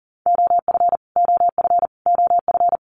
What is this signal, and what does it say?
Morse code, heard as a keyed tone, about 40 wpm, 700 Hz. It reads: OF OF OF
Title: Top 500 Words Repeated 40wpm